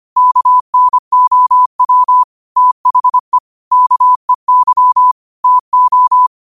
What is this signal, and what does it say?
Morse code, heard as a keyed tone, about 25 wpm, 1000 Hz. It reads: KNOW THE KEY TO
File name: 25wpm